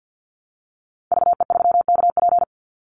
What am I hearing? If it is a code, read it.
Morse code, heard as a keyed tone, about 50 wpm, 700 Hz. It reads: VE3KP
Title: Call Signs Repeated 3x - 50wpm